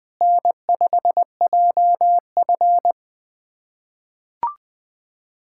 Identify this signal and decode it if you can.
Morse code, heard as a keyed tone, about 21 wpm, 700 Hz. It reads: N5JF
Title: USA Callsign DE Callsign - 4 Characters - 20wpm